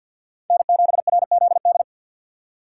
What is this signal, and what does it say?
Morse code, heard as a keyed tone, about 52 wpm, 700 Hz. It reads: N7RZD
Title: USA Call Signs - 5 Characters - 6x Character Spacing - 50wpm